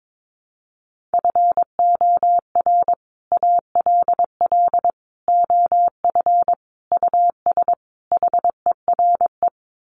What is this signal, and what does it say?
Morse code, heard as a keyed tone, about 22 wpm, 700 Hz. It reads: FOR ALL OF US HERE
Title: Sets of 5 Words 22wpm